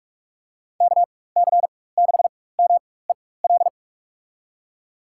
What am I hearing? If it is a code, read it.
Morse code, heard as a keyed tone, about 47 wpm, spaced as for 21 wpm, 700 Hz. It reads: KC6MEL
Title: USA Call Signs - 4x Character Spacing - 45wpm